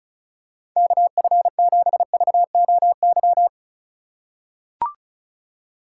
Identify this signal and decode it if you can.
Morse code, heard as a keyed tone, about 34 wpm, 700 Hz. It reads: KF7VOY
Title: USA Call Signs - 2x Character Spacing - 35wpm